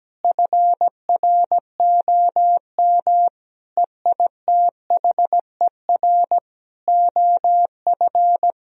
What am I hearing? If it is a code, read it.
Morse code, heard as a keyed tone, about 17 wpm, 700 Hz. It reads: FROM EITHER OF
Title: Sentences from Top 1000 Words - Review Part 4 - 17wpm